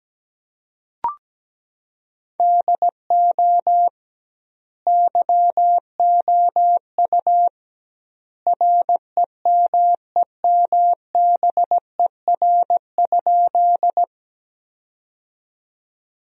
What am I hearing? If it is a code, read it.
Morse code, heard as a keyed tone, about 17 wpm, 700 Hz. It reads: DO YOU REMEMBER?
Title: 2x Word-Spacing - Sentences from Top 400 Words - 17wpm